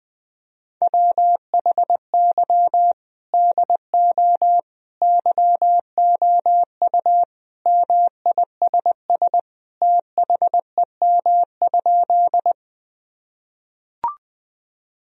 Morse code, heard as a keyed tone, about 21 wpm, 700 Hz. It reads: WHY DO YOU MISS THEM?
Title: Sentences from Top 1000 Words - Review Part 3 - 20wpm